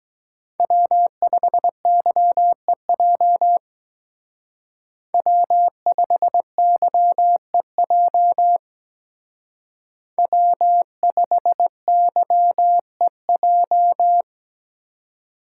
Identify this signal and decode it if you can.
Morse code, heard as a keyed tone, about 21 wpm, 700 Hz. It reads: W5YEJ W5YEJ W5YEJ
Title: Speed Racing - USA 1x3 Call Signs 17wpm